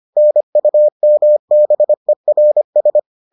Morse code, heard as a keyed tone, about 25 wpm, 600 Hz. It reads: NUMBERS